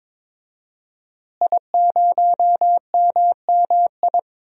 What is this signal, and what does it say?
Morse code, heard as a keyed tone, about 22 wpm, 700 Hz. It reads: I0MMI
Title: DXCC-100 Call Signs - 5 Characters - Repeated 3x - 22wpm